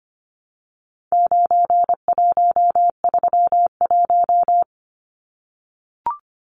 Morse code, heard as a keyed tone, about 25 wpm, 700 Hz. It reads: 9131
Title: Numbers - 4-Digit 25wpm